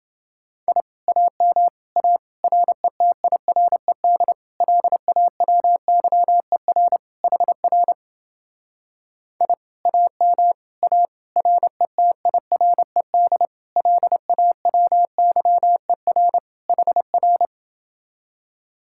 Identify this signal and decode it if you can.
Morse code, heard as a keyed tone, about 28 wpm, 700 Hz. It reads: I AM A RETIRED LAWYER HR I AM A RETIRED LAWYER HR